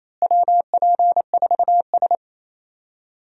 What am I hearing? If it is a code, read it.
Morse code, heard as a keyed tone, about 28 wpm, 700 Hz. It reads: WP4S
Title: USA Call Signs - 4 Characters - Repeated 3x - 28wpm